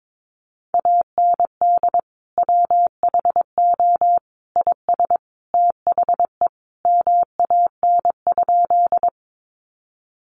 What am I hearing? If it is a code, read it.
Morse code, heard as a keyed tone, about 22 wpm, 700 Hz. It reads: AND WHO IS THE MAN?
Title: Sentences from Top 200 Words 22wpm